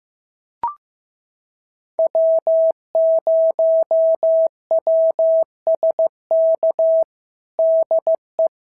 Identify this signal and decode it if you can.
Morse code, heard as a keyed tone, about 15 wpm, 650 Hz. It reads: W0WSK DE
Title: SOTA QSO Practice 15wpm